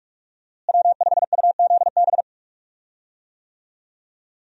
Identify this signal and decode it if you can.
Morse code, heard as a keyed tone, about 46 wpm, 700 Hz. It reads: W5UZB
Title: USA Call Signs - 5 Characters - Repeated 3x - 45wpm